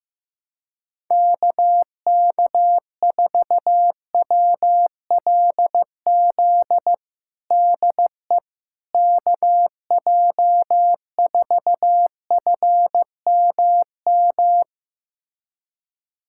Morse code, heard as a keyed tone, about 15 wpm, 700 Hz. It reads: KK4WLZ DE KJ4FMM
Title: USA Callsign DE Callsign - 6 Characters - 15wpm